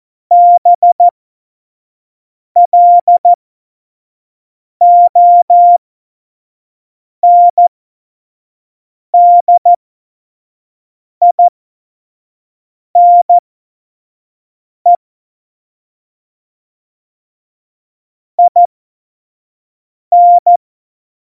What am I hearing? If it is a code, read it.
Morse code, heard as a keyed tone, about 14 wpm, spaced as for 5 wpm, 700 Hz. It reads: BLONDINE IN
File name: Morsekurs / Witze / Witz 25 Juli 17, Tempo 25 F70